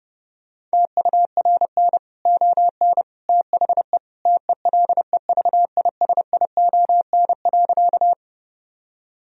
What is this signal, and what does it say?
Morse code, heard as a keyed tone, about 30 wpm, 700 Hz. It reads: TURN ON THE TELEVISION.
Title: Sentences from Top 1000 Words - Review Part 4 - 30wpm